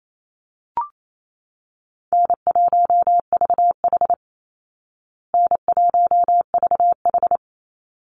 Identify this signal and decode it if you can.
Morse code, heard as a keyed tone, about 28 wpm, 700 Hz. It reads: N1VH N1VH
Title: Easy Call Signs Repeated 2x - 28wpm